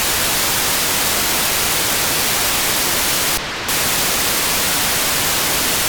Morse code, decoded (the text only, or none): none